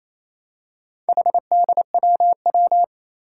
Morse code, heard as a keyed tone, about 29 wpm, 700 Hz. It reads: HDWW
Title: Four Challenging Letter-Number Combinations 28wpm